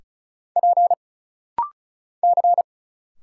Morse code, heard as a keyed tone, about 35 wpm, 700 Hz. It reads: P C
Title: Single Letters - Rapid-Fire 35wpm